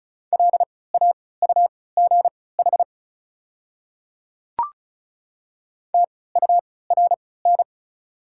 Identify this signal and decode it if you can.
Morse code, heard as a keyed tone, about 36 wpm, spaced as for 20 wpm, 690 Hz. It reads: LAUGH TURN
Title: 3x Character Spacing - Top 500 Words - 35wpm